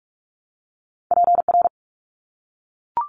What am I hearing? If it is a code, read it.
Morse code, heard as a keyed tone, about 44 wpm, 700 Hz. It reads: PR
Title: ARRL Sections 45wpm